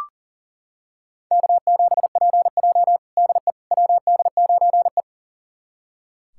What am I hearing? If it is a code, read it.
Morse code, heard as a keyed tone, about 40 wpm, 700 Hz. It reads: K7PJ DE WD9E
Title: USA Callsign DE Callsign - 4 Characters - 40wpm